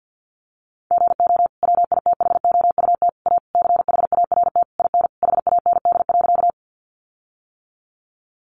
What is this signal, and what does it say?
Morse code, heard as a keyed tone, about 50 wpm, 700 Hz. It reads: GO WITHOUT A CHART IN HAND.
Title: Sentences from Top 1300 Words 50wpm